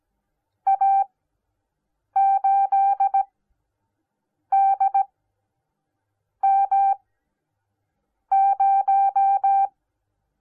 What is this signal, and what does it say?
Morse code, heard as a keyed tone, about 17 wpm, spaced as for 6 wpm, 780 Hz. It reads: A8DM0